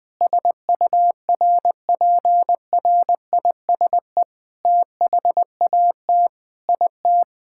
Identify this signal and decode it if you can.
Morse code, heard as a keyed tone, about 20 wpm, 700 Hz. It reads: SURPRISE THAT IT
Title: Sentences with 8-Word Phrases - 20wpm